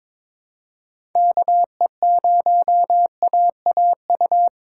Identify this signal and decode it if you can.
Morse code, heard as a keyed tone, about 22 wpm, 700 Hz. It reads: KE0AAU
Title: USA Call Signs - 2x Character Spacing - 22wpm